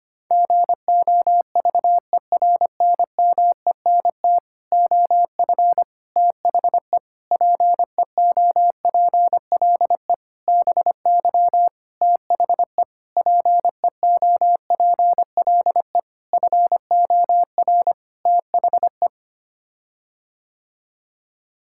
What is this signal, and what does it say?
Morse code, heard as a keyed tone, about 25 wpm, 700 Hz. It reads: GOVERNMENT OF THE PEOPLE BY THE PEOPLE FOR THE
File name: Sets of 9 Words - 25wpm